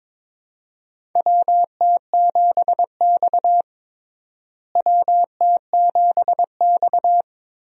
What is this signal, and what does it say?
Morse code, heard as a keyed tone, about 22 wpm, 700 Hz. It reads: WT7X WT7X
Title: USA Call Signs - 4 Characters - Repeated 3x - 22wpm